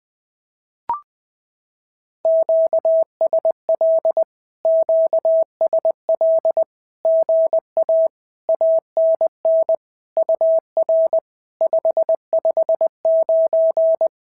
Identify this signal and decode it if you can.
Morse code, heard as a keyed tone, about 19 wpm, 650 Hz. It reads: QSL QSL GA ANN UR 559
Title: SOTA QSO Practice 20wpm